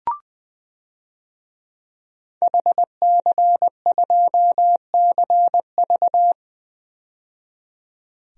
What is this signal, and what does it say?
Morse code, heard as a keyed tone, about 20 wpm, 700 Hz. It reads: HC2CV
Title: DXCC-100 Call Signs - 5 Characters - 20wpm